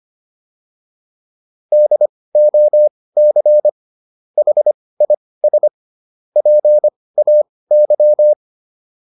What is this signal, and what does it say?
Morse code, heard as a keyed tone, about 25 wpm, spaced as for 18 wpm, 600 Hz. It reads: DOC HIS PAY